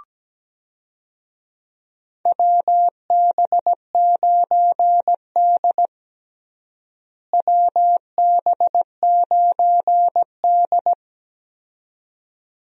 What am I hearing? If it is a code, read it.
Morse code, heard as a keyed tone, about 17 wpm, 700 Hz. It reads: WB9D WB9D
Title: USA Call Signs - 4 Characters - Repeated 2x - 17wpm